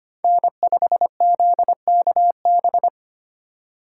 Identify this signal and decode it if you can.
Morse code, heard as a keyed tone, about 25 wpm, 700 Hz. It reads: N5ZKB
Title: USA Call Signs - 5 Characters - 3x Character Spacing - 25wpm